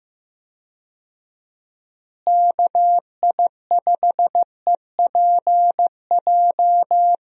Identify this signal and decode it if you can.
Morse code, heard as a keyed tone, about 15 wpm, 700 Hz. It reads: KI5EPJ
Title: USA Call Signs - 6 Characters - Repeated 2x - 15wpm